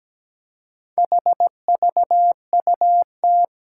Morse code, heard as a keyed tone, about 17 wpm, 700 Hz. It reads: HVUT
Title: Four Challenging Letter-Number Combinations 17wpm